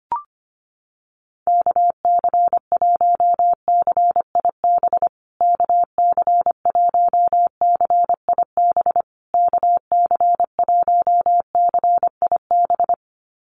Morse code, heard as a keyed tone, about 25 wpm, 700 Hz. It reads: KC1CIB KC1CIB KC1CIB
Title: USA Call Signs Repeated 3x 25wpm